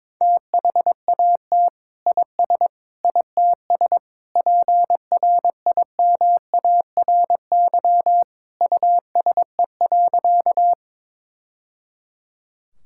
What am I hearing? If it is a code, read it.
Morse code, heard as a keyed tone, about 22 wpm, 700 Hz. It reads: THAT IS ITS PRIMARY USE.
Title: Sentences from Top 1000 Words - Review Part 5 - 22wpm